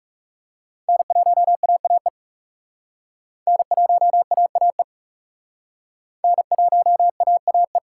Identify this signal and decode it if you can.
Morse code, heard as a keyed tone, about 36 wpm, 700 Hz. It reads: N1AAE N1AAE N1AAE
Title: Speed Racing - USA 1x3 Call Signs 30wpm